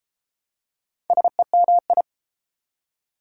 Morse code, heard as a keyed tone, about 32 wpm, 700 Hz. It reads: SEMI